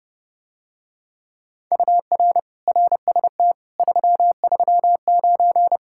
Morse code, heard as a keyed tone, about 30 wpm, 700 Hz. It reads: UR RST 339